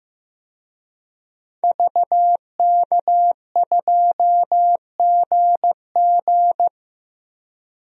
Morse code, heard as a keyed tone, about 15 wpm, 700 Hz. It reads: VK2GG
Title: DXCC-100 Call Signs - 5 Characters - 15wpm